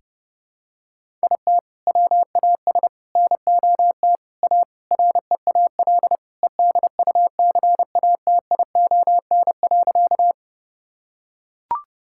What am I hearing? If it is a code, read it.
Morse code, heard as a keyed tone, about 30 wpm, 700 Hz. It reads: IT WAS NOT A REAL EDUCATION.
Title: Sentences from Top 1000 Words - Review Part 6 - 30wpm